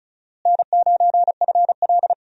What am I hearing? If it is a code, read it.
Morse code, heard as a keyed tone, about 36 wpm, 700 Hz. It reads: N9FL
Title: USA Call Signs - 4 Characters - Repeated 3x - 35wpm